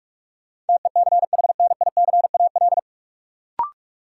Morse code, heard as a keyed tone, about 44 wpm, 700 Hz. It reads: TECHNICAL
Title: Sentences from Top 1100 Words - New Word Drill Alt - 45wpm